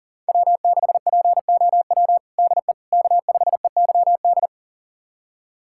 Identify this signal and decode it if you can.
Morse code, heard as a keyed tone, about 40 wpm, 700 Hz. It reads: W6POW DE K5EYD
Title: USA Callsign DE Callsign 40wpm